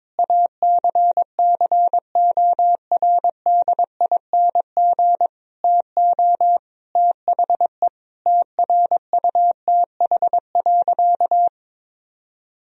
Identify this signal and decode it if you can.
Morse code, heard as a keyed tone, about 21 wpm, 700 Hz. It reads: ACCORDING TO THE TRUTH.
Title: Sentences from Top 1000 Words - Review Part 4 - 22wpm